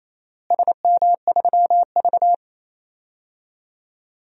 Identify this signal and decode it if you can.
Morse code, heard as a keyed tone, about 28 wpm, 700 Hz. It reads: SM3V
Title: DXCC-100 Call Signs - 4 Characters - Repeated 3x - 28wpm